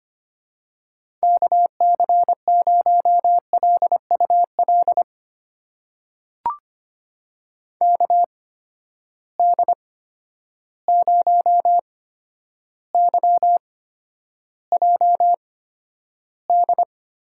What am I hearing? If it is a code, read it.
Morse code, heard as a keyed tone, about 25 wpm, spaced as for 11 wpm, 700 Hz. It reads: KC0LUL KD0YJD
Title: USA Call Signs - 6 Characters - 8x Character Spacing - 25wpm